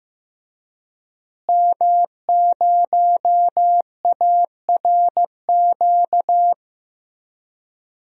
Morse code, heard as a keyed tone, about 16 wpm, 700 Hz. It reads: M0ARQ
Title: DXCC-100 Call Signs - 5 Characters - Repeated 2x - 15wpm